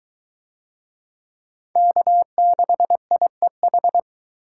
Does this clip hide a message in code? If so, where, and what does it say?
Yes: Morse code from 1.8 to 4.0 s, heard as a keyed tone, about 21 wpm, 700 Hz. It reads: K6IEH